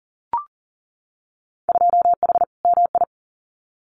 Morse code, heard as a keyed tone, about 40 wpm, 700 Hz. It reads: 2H MI
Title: Winter Field Day Exchanges 40wpm